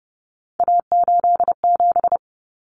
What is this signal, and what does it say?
Morse code, heard as a keyed tone, about 30 wpm, 700 Hz. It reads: A87
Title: Common Cut Numbers - 3-Digit 30wpm